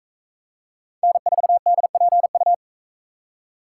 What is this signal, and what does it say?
Morse code, heard as a keyed tone, about 43 wpm, 700 Hz. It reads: N4DPU